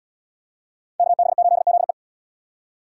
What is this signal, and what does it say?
Morse code, heard as a keyed tone, about 74 wpm, 700 Hz. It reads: KB8ZE